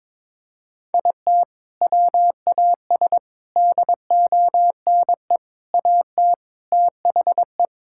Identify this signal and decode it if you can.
Morse code, heard as a keyed tone, about 22 wpm, 700 Hz. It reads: IT WAS DONE AT THE
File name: practice / 22wpm / Sentences with 4-Word Phrases - 22wpm